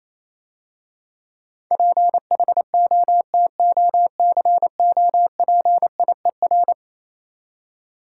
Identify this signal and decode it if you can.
Morse code, heard as a keyed tone, about 27 wpm, 690 Hz. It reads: PHOTOCOPIER
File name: Easy Words - Speed-Racing with Hints - 22wpm